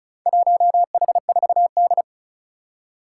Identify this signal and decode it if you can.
Morse code, heard as a keyed tone, about 36 wpm, 700 Hz. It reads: 1H4D
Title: Four Challenging Letter-Number Combinations 35wpm